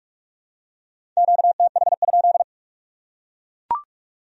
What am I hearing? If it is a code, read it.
Morse code, heard as a keyed tone, about 44 wpm, 700 Hz. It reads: QTH?